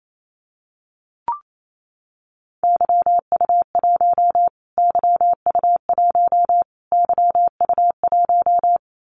Morse code, heard as a keyed tone, about 28 wpm, 700 Hz. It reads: YU1 YU1 YU1
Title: Callsign Prefixes for Top 25 DXCC - Repeated 3x - 28wpm